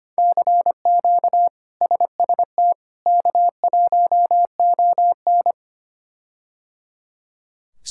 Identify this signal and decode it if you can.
Morse code, heard as a keyed tone, about 25 wpm, 700 Hz. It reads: CQ SST K1ON
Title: K1USN SST QSO Practice - 25wpm